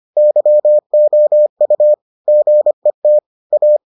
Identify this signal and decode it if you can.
Morse code, heard as a keyed tone, about 25 wpm, 600 Hz. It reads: YOU GET A